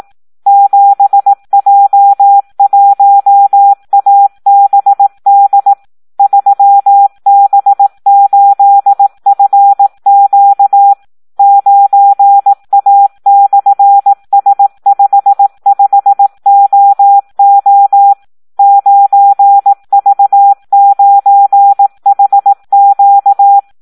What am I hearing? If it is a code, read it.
Morse code, heard as a keyed tone, about 18 wpm, 800 Hz. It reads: 7J1ABD 3B8FQ 9A/S55OO 9V9HQ
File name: Audio practica CW / SD 1 18